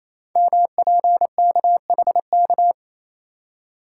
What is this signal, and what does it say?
Morse code, heard as a keyed tone, about 27 wpm, 700 Hz. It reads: MPKHK
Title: Code Groups - 5 Letter:Numbers - 28wpm